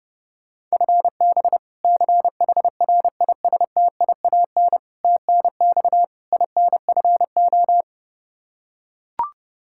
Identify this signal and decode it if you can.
Morse code, heard as a keyed tone, about 30 wpm, 700 Hz. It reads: FB CHRISTIAN TNX INFO
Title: K1USN SST QSO Practice 30wpm